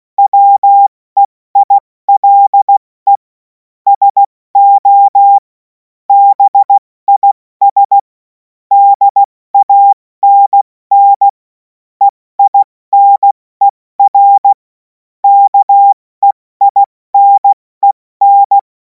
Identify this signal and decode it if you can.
Morse code, heard as a keyed tone, about 16 wpm, spaced as for 14 wpm, 800 Hz. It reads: WEILE SO BIS DANN EINER KEINEN
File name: Morsekurs / Witze / Witz 17. März 2026 70 F80